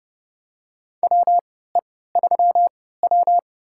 Morse code, heard as a keyed tone, about 30 wpm, spaced as for 17 wpm, 700 Hz. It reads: WE3W
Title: USA Call Signs - 4 Characters - 3x Character Spacing - 30wpm